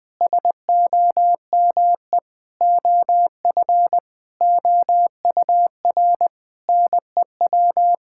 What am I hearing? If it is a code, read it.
Morse code, heard as a keyed tone, about 20 wpm, 700 Hz. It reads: SOME OF OUR NEW
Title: Sentences with 9-Word Phrases - Spoken Beginning - 20wpm